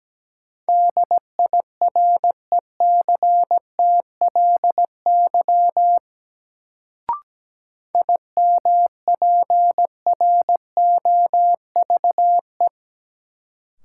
Morse code, heard as a keyed tone, about 17 wpm, 700 Hz. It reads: DIRECTLY IMPROVE
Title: Sentences from Top 800 Words - New Word Drill - 17wpm